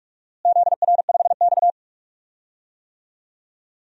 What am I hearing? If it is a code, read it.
Morse code, heard as a keyed tone, about 43 wpm, 700 Hz. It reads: ZR5X
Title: DXCC-100 Call Signs - 4 Characters - Repeated 3x - 45wpm